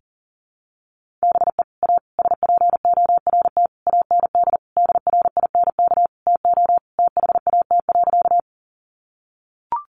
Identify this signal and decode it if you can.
Morse code, heard as a keyed tone, about 40 wpm, 700 Hz. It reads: BE A SPORT AND DRINK TO THAT.
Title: Sentences from Top 1500 Words 40wpm